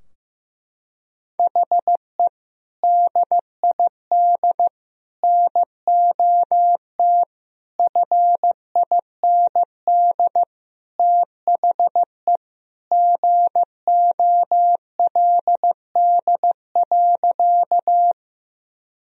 Morse code, heard as a keyed tone, about 16 wpm, 700 Hz. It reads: HE DID NOT FIND THE GOLD.
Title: Sentences from Top 1000 Words - Review Part 3 - 15wpm